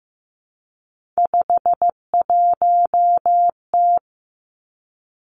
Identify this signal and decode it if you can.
Morse code, heard as a keyed tone, about 15 wpm, 700 Hz. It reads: 51T
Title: Three Challenging Letter-Number Combinations 15wpm